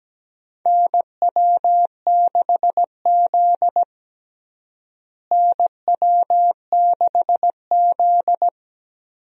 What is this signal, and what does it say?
Morse code, heard as a keyed tone, about 17 wpm, 700 Hz. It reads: NW6Z NW6Z
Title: USA Call Signs - 4 Characters - Repeated 3x - 17wpm